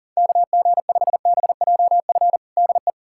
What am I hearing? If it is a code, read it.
Morse code, heard as a keyed tone, about 40 wpm, 700 Hz. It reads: KG5BJF DE
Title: USA Callsign DE Callsign - 6 Characters - 40wpm